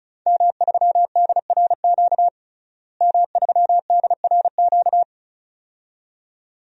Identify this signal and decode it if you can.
Morse code, heard as a keyed tone, about 34 wpm, 700 Hz. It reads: M3DRQ M3DRQ
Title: DXCC-100 Call Signs - 5 Characters - Repeated 2x - 35wpm